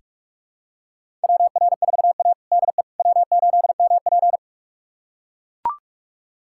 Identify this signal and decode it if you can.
Morse code, heard as a keyed tone, about 44 wpm, 700 Hz. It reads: WR4A DE W8MP
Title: USA Callsign DE Callsign - 4 Characters - 45wpm